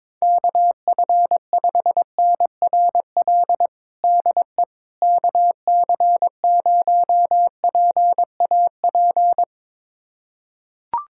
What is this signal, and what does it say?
Morse code, heard as a keyed tone, about 22 wpm, 700 Hz. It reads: KF5NRL DE KC0PAP
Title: USA Callsign DE Callsign 22wpm